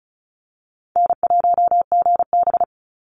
Morse code, heard as a keyed tone, about 36 wpm, 700 Hz. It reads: N1GB
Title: NA SOTA Chaser Callsigns 35wpm